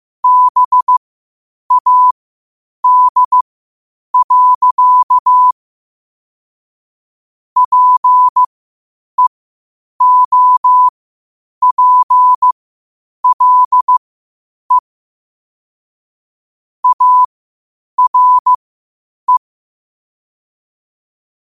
Morse code, heard as a keyed tone, about 15 wpm, spaced as for 8 wpm, 1000 Hz. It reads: BAD. PEOPLE ARE
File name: qotdfiles / 8wpm